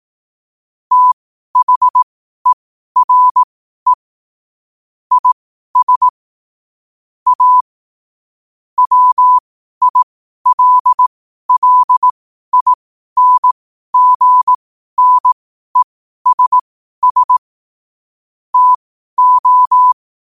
Morse code, heard as a keyed tone, about 18 wpm, spaced as for 12 wpm, 1000 Hz. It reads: THERE IS A WILLINGNESS TO
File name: qotdfiles / 12wpm